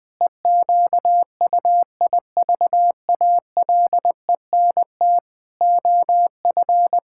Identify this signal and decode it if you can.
Morse code, heard as a keyed tone, about 21 wpm, 700 Hz. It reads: EQUIVALENT OF
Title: Sentences from Top 1600 Words 20wpm